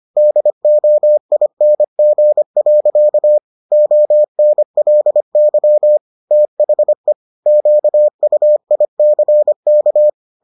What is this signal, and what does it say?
Morse code, heard as a keyed tone, about 25 wpm, 600 Hz. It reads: DOING. ONLY THE QUICK